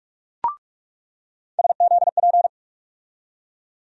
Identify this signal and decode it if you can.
Morse code, heard as a keyed tone, about 47 wpm, 700 Hz. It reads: SZP
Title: USA Call Sign Postfixes - 3 Letters 45wpm